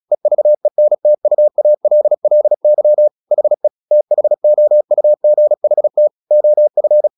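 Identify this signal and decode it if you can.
Morse code, heard as a keyed tone, about 36 wpm, 600 Hz. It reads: EVENTUALLY HE THOUGHT OF
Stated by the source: Morse practice at 36 wpm, text of Ken Follett's novel Never, from CW med Gnister.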